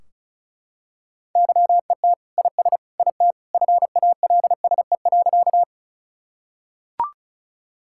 Morse code, heard as a keyed tone, about 34 wpm, 700 Hz. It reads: YET IS IT FALSE.
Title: Sentences from Top 1600 Words 35wpm